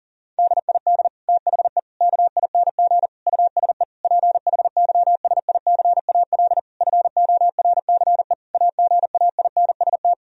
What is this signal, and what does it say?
Morse code, heard as a keyed tone, about 40 wpm, 700 Hz. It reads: DID THE KING USE PHYSICAL FORCE AGAINST
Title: Sentences from Top 1000 Words - Review Part 4 - 40wpm